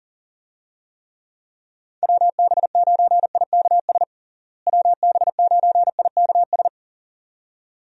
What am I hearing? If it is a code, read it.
Morse code, heard as a keyed tone, about 40 wpm, 700 Hz. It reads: WB9IKS WB9IKS
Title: USA Call Signs - 6 Characters - Repeated 2x - 40wpm